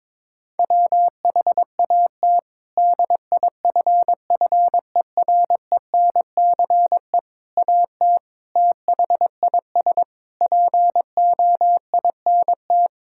Morse code, heard as a keyed tone, about 22 wpm, 700 Hz. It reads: WHAT DIFFERENCE AT THIS POINT
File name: practice / 22wpm / Sets of 8 Words - 22wpm